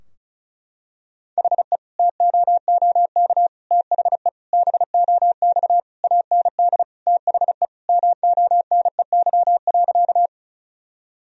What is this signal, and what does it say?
Morse code, heard as a keyed tone, about 34 wpm, 700 Hz. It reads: HE TOOK THE BOX AND THE MONEY.
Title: Sentences from Top 1000 Words - Review Part 2 - 35wpm